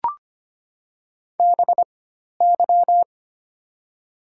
Morse code, heard as a keyed tone, about 25 wpm, spaced as for 12 wpm, 700 Hz. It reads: BY
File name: Top 90 QSO Elements - 4x Character Spacing - 25wpm